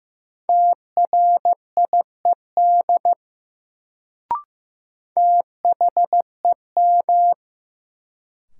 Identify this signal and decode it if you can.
Morse code, heard as a keyed tone, about 15 wpm, 700 Hz. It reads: TRIED THEM
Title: Sets of 2 Words - New Word Drill - 15wpm